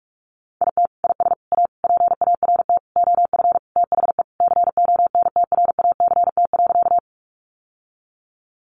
Morse code, heard as a keyed tone, about 44 wpm, 700 Hz. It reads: IT IS A PART OF THE CONTRACT.
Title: Sentences from Top 1400 Words 45wpm